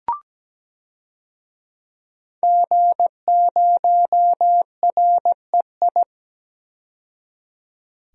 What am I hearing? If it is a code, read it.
Morse code, heard as a keyed tone, about 17 wpm, 700 Hz. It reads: G0REI
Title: DXCC-100 Call Signs - 5 Characters - 17wpm